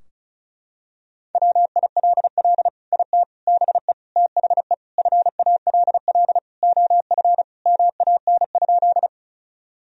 Morse code, heard as a keyed tone, about 36 wpm, 700 Hz. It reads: WILL IT BE THE FALL OF MAN?
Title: Sentences from Top 1000 Words - Review Part 2 - 35wpm